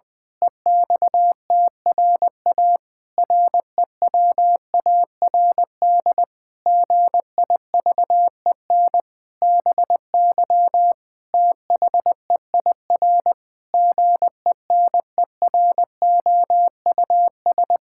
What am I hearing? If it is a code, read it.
Morse code, heard as a keyed tone, about 21 wpm, 700 Hz. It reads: EXTRA REWARD GIVEN BY THEIR GENEROUS
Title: Pangram Challenge v2 - 20wpm